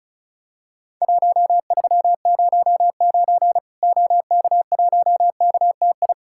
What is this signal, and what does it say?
Morse code, heard as a keyed tone, about 35 wpm, 700 Hz. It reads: 1309 OK1KTI
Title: Worked All Europe DX Contest - QTC Exchange - 35wpm